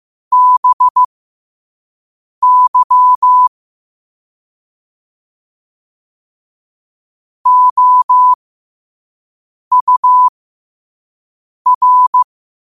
Morse code, heard as a keyed tone, about 15 wpm, spaced as for 5 wpm, 1000 Hz. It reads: BY OUR
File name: 5wpm